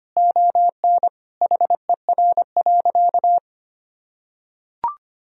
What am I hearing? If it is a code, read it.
Morse code, heard as a keyed tone, about 25 wpm, 700 Hz. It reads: ON HER.
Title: Sentences from Top 1000 Words - Review Part 1 - 25wpm